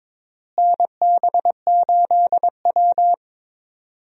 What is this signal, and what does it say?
Morse code, heard as a keyed tone, about 22 wpm, 700 Hz. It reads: NB8W
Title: USA Call Signs - 4 Characters - Repeated 2x - 22wpm